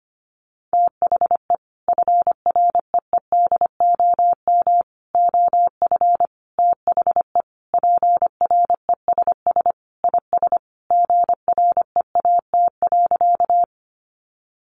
Morse code, heard as a keyed tone, about 25 wpm, 700 Hz. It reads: THE FREEDOM OF THE PRESS IS GREAT.
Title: Sentences from Top 800 Words 25wpm